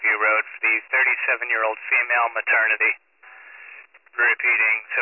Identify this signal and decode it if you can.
Emergency Alert System (SAME) tones (header bursts), heard as harsh buzzing bursts, with speech. It is heard from 0.5 to 4.8 s.